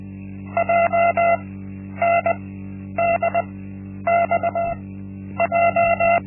Morse code, heard as a keyed tone, about 20 wpm, spaced as for 10 wpm, 680 Hz. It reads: JNDXJ
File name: Linksmakalnis-1